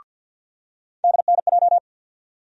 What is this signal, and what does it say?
Morse code, heard as a keyed tone, about 53 wpm, 700 Hz. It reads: DNJ
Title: USA Call Sign Postfixes - 3 Letters 50wpm